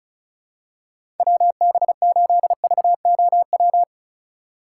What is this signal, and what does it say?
Morse code, heard as a keyed tone, about 34 wpm, 700 Hz. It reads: WB8VOW
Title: USA Call Signs - 6 Characters - 2x Character Spacing - 35wpm